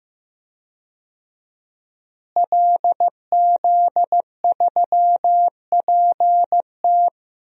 Morse code, heard as a keyed tone, about 15 wpm, 700 Hz. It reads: LZ3PT
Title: DXCC-100 Call Signs - 5 Characters - Repeated 3x - 15wpm